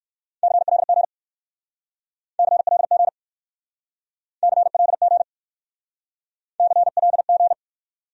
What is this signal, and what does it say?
Morse code, heard as a keyed tone, about 60 wpm, 700 Hz. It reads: CLG CLG CLG CLG